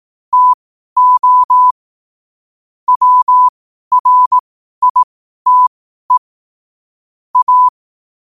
Morse code, heard as a keyed tone, about 18 wpm, spaced as for 12 wpm, 1000 Hz. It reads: TO WRITE A
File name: 12wpm